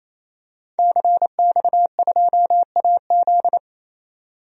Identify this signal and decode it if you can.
Morse code, heard as a keyed tone, about 28 wpm, 700 Hz. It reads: CX2AZ
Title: DXCC-100 Call Signs - 5 Characters - 28wpm